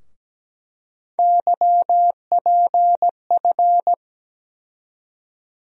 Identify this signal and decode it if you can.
Morse code, heard as a keyed tone, about 17 wpm, 700 Hz. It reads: YPF
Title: USA Call Sign Postfixes - 3 Letters 17wpm